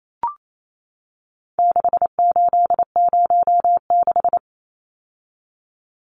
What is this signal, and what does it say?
Morse code, heard as a keyed tone, about 28 wpm, 700 Hz. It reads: 6806
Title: Numbers - 4-Digit 28wpm